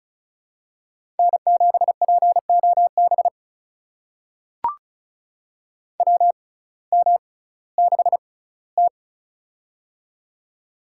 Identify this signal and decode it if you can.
Morse code, heard as a keyed tone, about 36 wpm, spaced as for 19 wpm, 700 Hz. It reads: N7POB WM6T
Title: USA Call Signs - 6x Character Spacing - 35wpm